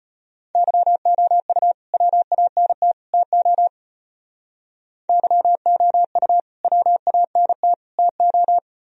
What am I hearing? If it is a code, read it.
Morse code, heard as a keyed tone, about 34 wpm, 700 Hz. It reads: YOU WANT TO YOU WANT TO